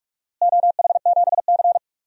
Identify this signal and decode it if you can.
Morse code, heard as a keyed tone, about 45 wpm, 700 Hz. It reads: OH7C
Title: DXCC-100 Call Signs - 4 Characters - Repeated 3x - 45wpm